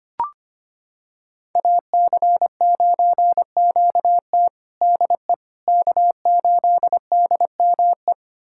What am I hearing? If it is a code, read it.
Morse code, heard as a keyed tone, about 25 wpm, 700 Hz. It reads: AC9QT DE K8DME
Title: USA Callsign DE Callsign - 5 Characters - 25wpm